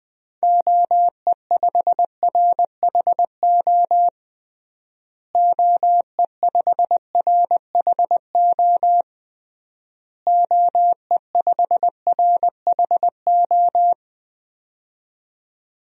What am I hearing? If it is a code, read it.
Morse code, heard as a keyed tone, about 21 wpm, 700 Hz. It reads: OE5RHO OE5RHO OE5RHO
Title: DXCC-100 Call Signs - 6 Characters - Repeated 3x - 20wpm